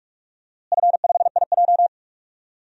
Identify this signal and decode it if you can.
Morse code, heard as a keyed tone, about 41 wpm, 700 Hz. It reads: F5IJ
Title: DXCC-100 Call Signs - 4 Characters - Repeated 2x - 45wpm